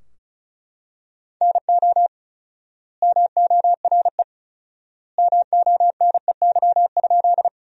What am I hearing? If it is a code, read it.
Morse code, heard as a keyed tone, about 36 wpm, 700 Hz. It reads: NO MORE MONEY?
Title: 4x Word-Spacing - Sentences from Top 400 Words - 35wpm